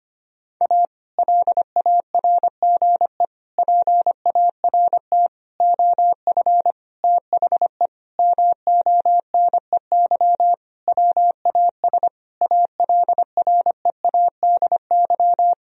Morse code, heard as a keyed tone, about 25 wpm, 700 Hz. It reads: A LARGE PART OF THE MONEY WAS ALREADY
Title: Sentences from Top 1000 Words - Review Part 1 - 25wpm